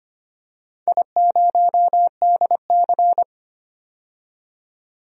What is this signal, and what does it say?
Morse code, heard as a keyed tone, about 25 wpm, 700 Hz. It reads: I0DC
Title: DXCC-100 Call Signs - 4 Characters - 25wpm